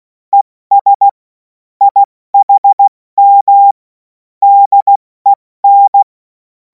Morse code, heard as a keyed tone, about 16 wpm, spaced as for 14 wpm, 800 Hz. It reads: ES IHM DEN
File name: Morsekurs / Witze / Witz 17. Februar 2026 70 F80